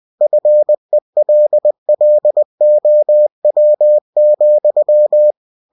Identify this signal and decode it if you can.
Morse code, heard as a keyed tone, about 20 wpm, 600 Hz. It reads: FELLOW,
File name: cw-books / winnie-the-pooh / winnie-20wpm-page-0019